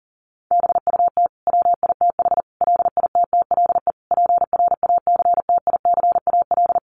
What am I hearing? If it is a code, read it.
Morse code, heard as a keyed tone, about 40 wpm, 700 Hz. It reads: BUT WITH LITTLE PRACTICAL
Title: Sentences from Top 1400 Words 40wpm